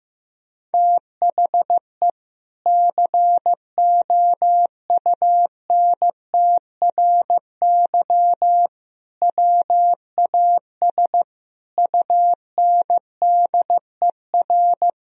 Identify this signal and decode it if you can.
Morse code, heard as a keyed tone, about 15 wpm, 700 Hz. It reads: THE COUNTRY WAS UNDER
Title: Sentences from Top 1000 Words - Review Part 1 - 15wpm